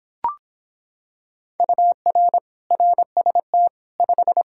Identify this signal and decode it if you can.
Morse code, heard as a keyed tone, about 25 wpm, 700 Hz. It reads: UR RST 5